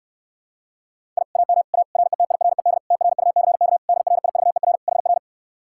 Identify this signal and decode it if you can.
Morse code, heard as a keyed tone, about 69 wpm, 700 Hz. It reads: I AM A RETIRED TRUCK DRIVER HR